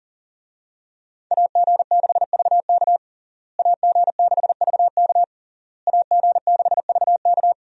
Morse code, heard as a keyed tone, about 37 wpm, 700 Hz. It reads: AG6VK AG6VK AG6VK
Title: USA Call Signs - 5 Characters - Repeated 3x - 40wpm